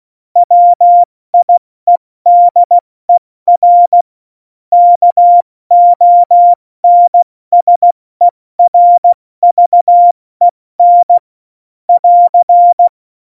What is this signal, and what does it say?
Morse code, heard as a keyed tone, about 16 wpm, spaced as for 14 wpm, 700 Hz. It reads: WIEDER KONSERVEN <AR>
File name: Morsekurs / Witze / Witz 15. Mai 2018, 70 F80